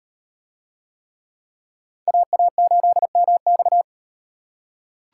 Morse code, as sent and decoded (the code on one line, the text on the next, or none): .- .- ---.. -- -..-
AA8MX